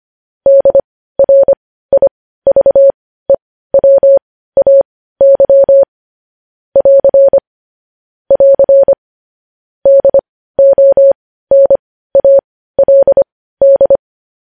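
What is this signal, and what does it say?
Morse code, heard as a keyed tone, about 25 wpm, spaced as for 15 wpm, 550 Hz. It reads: DRIVEWAY <AR> <AR> DONALD
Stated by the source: Morse code practice of news headlines at 15 wpm, opening with vvv.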